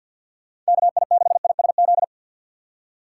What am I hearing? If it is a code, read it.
Morse code, heard as a keyed tone, about 53 wpm, 700 Hz. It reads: KI6ISZ
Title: USA Call Signs - 6x Character Spacing - 50wpm